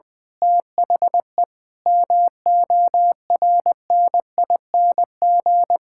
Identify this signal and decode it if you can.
Morse code, heard as a keyed tone, about 20 wpm, 700 Hz. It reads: THE MORNING
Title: Pangram Challenge v2 - 20wpm